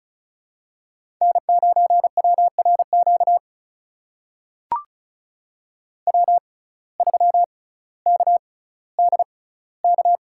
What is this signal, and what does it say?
Morse code, heard as a keyed tone, about 36 wpm, spaced as for 18 wpm, 700 Hz. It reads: N9WRQ W3KDK
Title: USA Call Signs - 5 Characters - 6x Character Spacing - 35wpm